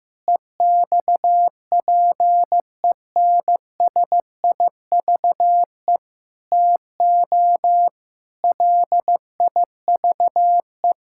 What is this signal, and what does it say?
Morse code, heard as a keyed tone, about 15 wpm, 700 Hz. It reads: EXPENSIVE TO LIVE
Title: Sentences from Top 1000 Words - Review Part 6 - 15wpm